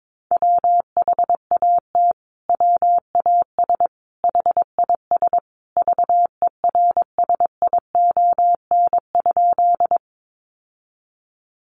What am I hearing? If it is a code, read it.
Morse code, heard as a keyed tone, about 22 wpm, 700 Hz. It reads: WHAT WAS HIS VERSION?
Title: Sentences from Top 1400 Words 22wpm